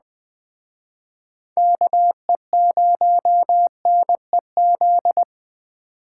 Morse code, heard as a keyed tone, about 20 wpm, 700 Hz. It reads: KE0NEZ